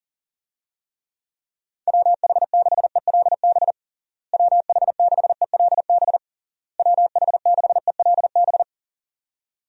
Morse code, heard as a keyed tone, about 37 wpm, 700 Hz. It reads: WH6ELB WH6ELB WH6ELB
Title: USA Call Signs - 6 Characters - Repeated 3x - 40wpm